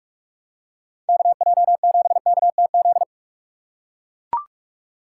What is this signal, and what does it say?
Morse code, heard as a keyed tone, about 43 wpm, 700 Hz. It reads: KJ7KTZ
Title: USA Call Signs - 6 Characters - 3x Character Spacing - 45wpm